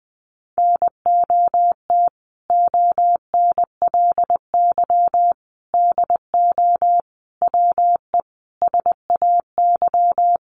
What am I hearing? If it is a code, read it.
Morse code, heard as a keyed tone, about 20 wpm, 700 Hz. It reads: NOT ONLY DO WE SAY
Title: Sentences from Top 1200 Words 20wpm